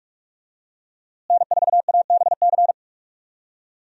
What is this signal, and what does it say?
Morse code, heard as a keyed tone, about 41 wpm, 700 Hz. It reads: N4ABC
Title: USA Call Signs - 5 Characters - Repeated 3x - 45wpm